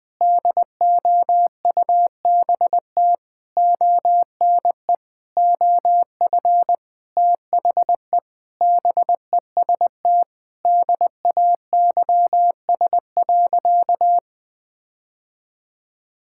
Morse code, heard as a keyed tone, about 20 wpm, 700 Hz. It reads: DOUBT ONE OF THE BEST DAYS.
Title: Sentences with 7-Word Phrases - Spoken Beginning - 20wpm